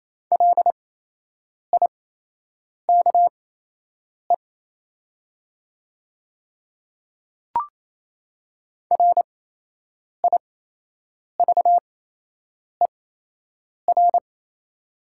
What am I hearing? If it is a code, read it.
Morse code, heard as a keyed tone, about 28 wpm, spaced as for 8 wpm, 700 Hz. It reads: LIKE RIVER
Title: 8x Character Spacing - Top 300 Words - 28wpm